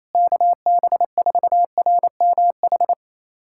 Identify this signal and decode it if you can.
Morse code, heard as a keyed tone, about 28 wpm, 700 Hz. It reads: KB4RMH
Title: USA Callsign DE Callsign 28wpm